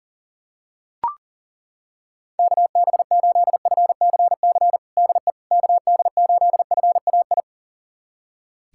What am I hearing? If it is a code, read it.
Morse code, heard as a keyed tone, about 40 wpm, 700 Hz. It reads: KB8FCC DE KD8FAI
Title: USA Callsign DE Callsign 40wpm